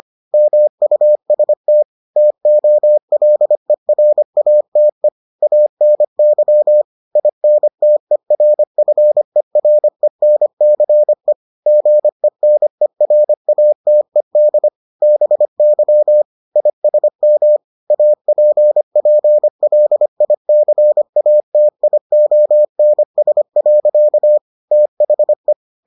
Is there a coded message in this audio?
Yes: Morse code from 0.3 to 25.5 s, heard as a keyed tone, about 25 wpm, 600 Hz. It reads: MUST TOLERATE ANY INTERFERENCE GENERATED BY ISM APPLICATIONS. THE